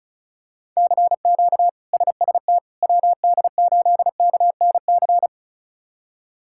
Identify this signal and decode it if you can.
Morse code, heard as a keyed tone, about 36 wpm, 700 Hz. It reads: CQ SST WD8KNC
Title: K1USN SST QSO Practice - 35wpm